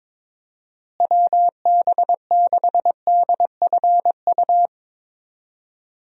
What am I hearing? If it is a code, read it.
Morse code, heard as a keyed tone, about 22 wpm, 700 Hz. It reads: WB6DFU
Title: USA Call Signs - 6 Characters - Repeated 2x - 22wpm